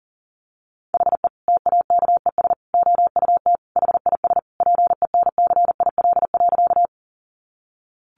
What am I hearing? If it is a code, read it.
Morse code, heard as a keyed tone, about 40 wpm, 700 Hz. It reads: HE TAKES OUT HIS PENCIL.
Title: Additional Easy Phrases 40wpm